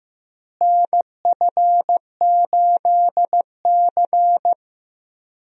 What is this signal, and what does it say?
Morse code, heard as a keyed tone, about 15 wpm, 700 Hz. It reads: NF8C
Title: USA Call Signs - 4 Characters - Repeated 3x - 15wpm